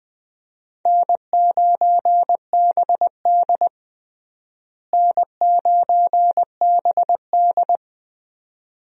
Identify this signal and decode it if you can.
Morse code, heard as a keyed tone, about 20 wpm, 700 Hz. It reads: N9BD N9BD
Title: USA Call Signs - 4 Characters - Repeated 2x - 20wpm